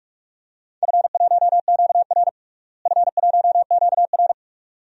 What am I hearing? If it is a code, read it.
Morse code, heard as a keyed tone, about 46 wpm, 700 Hz. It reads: F1QR F1QR
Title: DXCC-100 Call Signs - 4 Characters - Repeated 2x - 45wpm